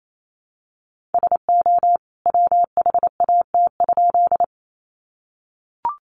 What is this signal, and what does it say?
Morse code, heard as a keyed tone, about 28 wpm, 700 Hz. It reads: SO WHAT?
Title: A Sound Of Thunder 28wpm